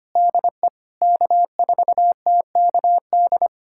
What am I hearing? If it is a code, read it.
Morse code, heard as a keyed tone, about 25 wpm, 700 Hz. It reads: DE K4TKD
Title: USA Callsign DE Callsign - 5 Characters - 25wpm